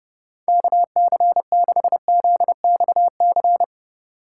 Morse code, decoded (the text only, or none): KC6ZXC